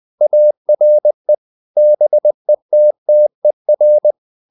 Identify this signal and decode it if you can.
Morse code, heard as a keyed tone, about 20 wpm, 600 Hz. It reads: ARE BETTER
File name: CW / Practice / EnglishSayings20